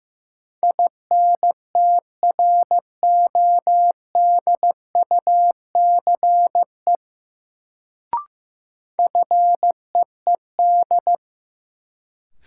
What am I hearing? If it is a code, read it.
Morse code, heard as a keyed tone, about 15 wpm, 700 Hz. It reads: INTRODUCE FEED
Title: Sentences from Top 1600 Words - New Word Drill Alt - 15wpm